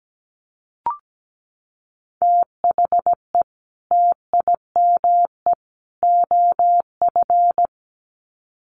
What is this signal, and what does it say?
Morse code, heard as a keyed tone, about 17 wpm, 700 Hz. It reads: THE TIME OF
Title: Sets of 3 Words 17wpm